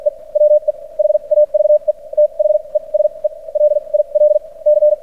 Morse code, heard as a keyed tone, about 53 wpm, spaced as for 37 wpm, 600 Hz. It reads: E WE HAVE THESE LIL C
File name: archived / CRSnet